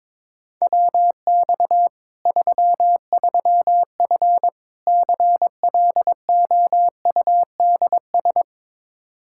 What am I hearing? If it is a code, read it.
Morse code, heard as a keyed tone, about 22 wpm, 700 Hz. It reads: WX 33F CLOUDS
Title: Practice QSO - Exchange 2 - Info - 22wpm